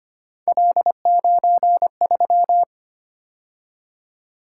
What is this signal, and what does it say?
Morse code, heard as a keyed tone, about 29 wpm, 700 Hz. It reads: L93
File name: Speed-Racing - Three Letter-Number Combinations - 25wpm